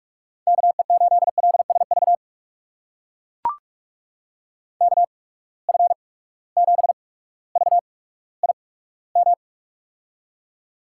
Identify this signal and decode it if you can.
Morse code, heard as a keyed tone, about 44 wpm, spaced as for 19 wpm, 700 Hz. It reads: KE8LSV KF7VIM
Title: USA Call Signs - 6 Characters - 8x Character Spacing - 45wpm